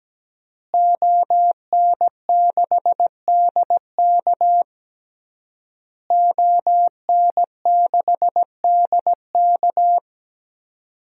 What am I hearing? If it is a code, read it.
Morse code, heard as a keyed tone, about 17 wpm, 700 Hz. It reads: ON6DK ON6DK
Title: DXCC-100 Call Signs - 5 Characters - Repeated 3x - 17wpm